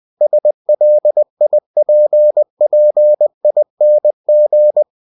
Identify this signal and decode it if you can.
Morse code, heard as a keyed tone, about 20 wpm, 600 Hz. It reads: SLIPPING
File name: cw-books / winnie-the-pooh / winnie-20wpm-page-0101